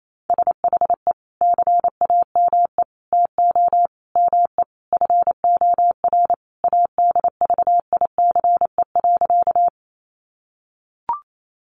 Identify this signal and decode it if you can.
Morse code, heard as a keyed tone, about 28 wpm, 700 Hz. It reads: SHE CAME TO ME FOR ADVICE.
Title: Sentences from Top 1100 Words 28wpm